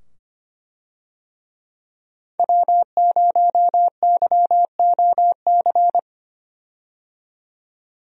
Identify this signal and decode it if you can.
Morse code, heard as a keyed tone, about 25 wpm, 700 Hz. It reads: W0YOC
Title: USA Call Signs - 5 Characters - 25wpm